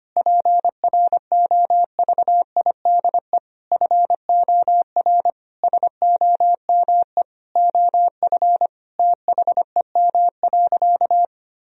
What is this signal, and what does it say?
Morse code, heard as a keyed tone, about 25 wpm, 700 Hz. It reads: PROVIDE FOR SOME OF THEM.
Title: Sentences from Top 1000 Words - Review Part 3 - 25wpm